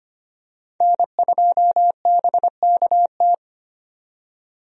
Morse code, heard as a keyed tone, about 25 wpm, 700 Hz. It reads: N2BKT
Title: USA Call Signs - 5 Characters - 25wpm